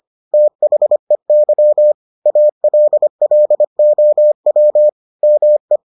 Morse code, heard as a keyed tone, about 25 wpm, 600 Hz. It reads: THEY ALLOW ME